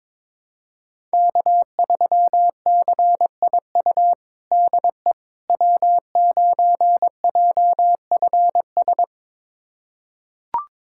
Morse code, heard as a keyed tone, about 21 wpm, 700 Hz. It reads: K3CIU DE W9JFS
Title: USA Callsign DE Callsign - 5 Characters - 22wpm